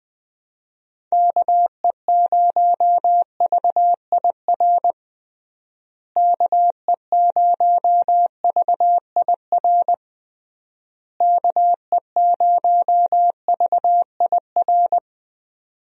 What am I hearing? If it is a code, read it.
Morse code, heard as a keyed tone, about 21 wpm, 700 Hz. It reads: KE0VIR KE0VIR KE0VIR
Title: USA Call Signs - 6 Characters - Repeated 3x - 20wpm